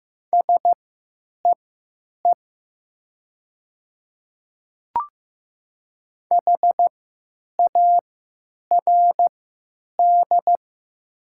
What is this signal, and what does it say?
Morse code, heard as a keyed tone, about 15 wpm, spaced as for 9 wpm, 700 Hz. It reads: SEE HARD
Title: 3x Character Spacing - Top 300 Words - 15wpm